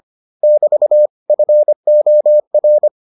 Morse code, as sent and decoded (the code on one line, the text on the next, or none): -...- / ..-. --- .-.
<BT> FOR